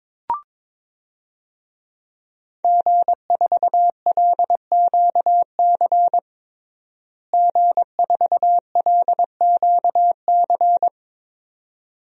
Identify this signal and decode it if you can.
Morse code, heard as a keyed tone, about 21 wpm, 700 Hz. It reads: G4LQC G4LQC
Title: DXCC-100 Call Signs - 5 Characters - Repeated 2x - 22wpm